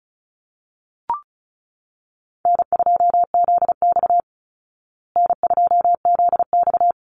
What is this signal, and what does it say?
Morse code, heard as a keyed tone, about 34 wpm, 700 Hz. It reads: N2ZX N2ZX
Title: Call Signs Repeated 3x - 35wpm